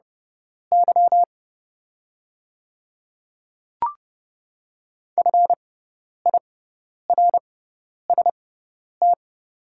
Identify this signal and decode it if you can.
Morse code, heard as a keyed tone, about 30 wpm, spaced as for 10 wpm, 700 Hz. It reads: Y FIRST
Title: 6x Character Spacing - Top 100 Words - 30wpm